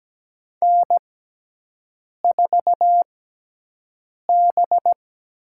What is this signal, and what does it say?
Morse code, heard as a keyed tone, about 17 wpm, spaced as for 6 wpm, 700 Hz. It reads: N4B
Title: USA Call Signs - 4 Characters - 6x Character Spacing - 17wpm